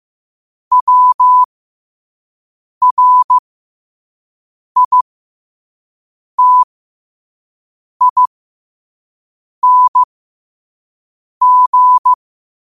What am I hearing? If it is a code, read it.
Morse code, heard as a keyed tone, about 15 wpm, spaced as for 5 wpm, 1000 Hz. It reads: WRITING
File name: qotdfiles / 5wpm